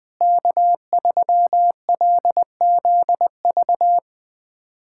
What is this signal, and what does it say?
Morse code, heard as a keyed tone, about 20 wpm, 700 Hz. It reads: K3LZV
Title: USA Callsign DE Callsign - 5 Characters - 20wpm